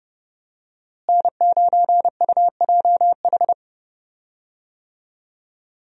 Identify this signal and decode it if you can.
Morse code, heard as a keyed tone, about 30 wpm, 700 Hz. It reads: N9UJH
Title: USA Call Signs - 5 Characters - Repeated 3x - 30wpm